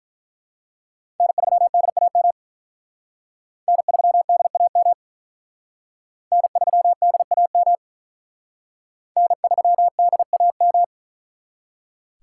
Morse code, heard as a keyed tone, about 43 wpm, 700 Hz. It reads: N3DAM N3DAM N3DAM N3DAM
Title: Speed Racing - USA 1x3 Call Signs 35wpm